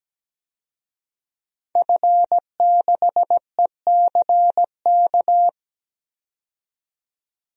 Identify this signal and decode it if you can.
Morse code, heard as a keyed tone, about 17 wpm, 700 Hz. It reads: F6ECK
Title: DXCC-100 Call Signs - 5 Characters - 17wpm